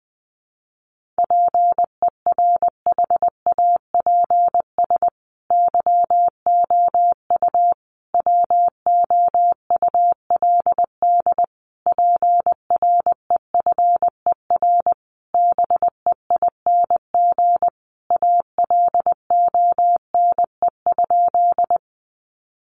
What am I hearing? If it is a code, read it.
Morse code, heard as a keyed tone, about 20 wpm, 700 Hz. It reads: PERHAPS YOU WOULD PREFER BEING ALONE?
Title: Sentences from Top 1100 Words 20wpm